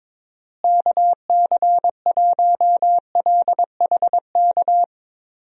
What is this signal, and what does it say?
Morse code, heard as a keyed tone, about 22 wpm, 700 Hz. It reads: KC1LHK
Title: USA Call Signs - 6 Characters - Repeated 3x - 22wpm